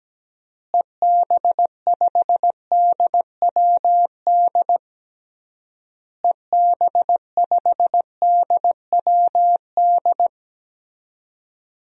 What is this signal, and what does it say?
Morse code, heard as a keyed tone, about 17 wpm, 700 Hz. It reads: EB5DWD EB5DWD
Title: DXCC-100 Call Signs - 6 Characters - Repeated 3x - 17wpm